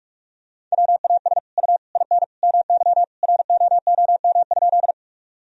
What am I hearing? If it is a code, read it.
Morse code, heard as a keyed tone, about 41 wpm, 700 Hz. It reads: WAS U IN MY ROOM?
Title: Sentences from Top 300 Words and Abbreviations 45wpm